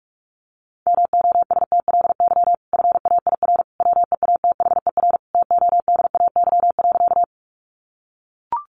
Morse code, heard as a keyed tone, about 44 wpm, 700 Hz. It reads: MOSTLY FAIR WEATHER TODAY.
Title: Sentences from Top 1300 Words 45wpm